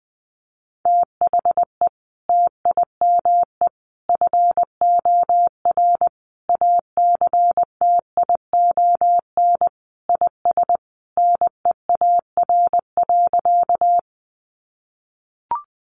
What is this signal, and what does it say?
Morse code, heard as a keyed tone, about 20 wpm, 700 Hz. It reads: THE TIME FOR ACTION IS NEAR.
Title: Sentences from Top 700 Words 20wpm